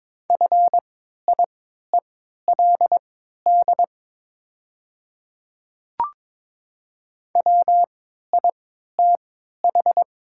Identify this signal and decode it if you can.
Morse code, heard as a keyed tone, about 22 wpm, spaced as for 12 wpm, 700 Hz. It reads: FIELD WITH
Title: 3x Character Spacing - Top 500 Words - 22wpm